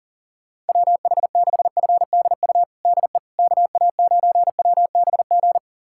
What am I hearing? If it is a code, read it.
Morse code, heard as a keyed tone, about 37 wpm, 700 Hz. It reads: WH6FDU DE KA9WBG
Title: USA Callsign DE Callsign - 6 Characters - 40wpm